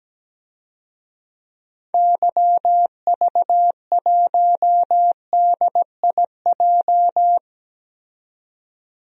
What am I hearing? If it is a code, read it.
Morse code, heard as a keyed tone, about 17 wpm, 700 Hz. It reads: YV1DIJ
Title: DXCC-100 Call Signs - 6 Characters - 17wpm